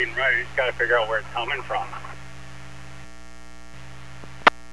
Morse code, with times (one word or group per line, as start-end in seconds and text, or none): none